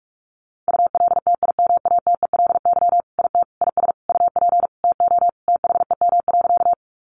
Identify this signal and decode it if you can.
Morse code, heard as a keyed tone, about 44 wpm, 700 Hz. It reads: ULTIMATELY IT IS UP TO THEM.
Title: Sentences from Top 1200 Words 45wpm